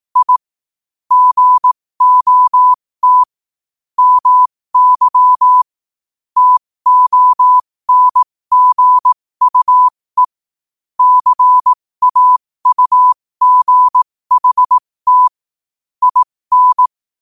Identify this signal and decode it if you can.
Morse code, heard as a keyed tone, about 18 wpm, spaced as for 15 wpm, 1000 Hz. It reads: I GOT MY TONGUE CAUGHT IN
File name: qotdfiles / 15wpm